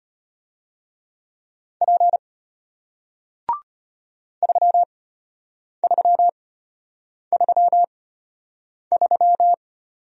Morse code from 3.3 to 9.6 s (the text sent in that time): 3333